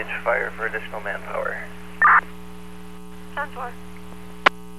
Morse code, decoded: none

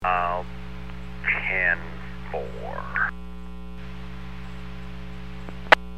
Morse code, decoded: none

